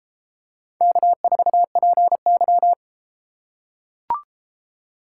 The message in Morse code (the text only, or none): K4PY